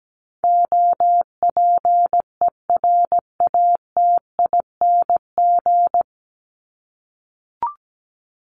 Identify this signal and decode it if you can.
Morse code, heard as a keyed tone, about 17 wpm, 700 Hz. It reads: OPERATING
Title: ING Suffix - Top 100 Words with Base Words 17wpm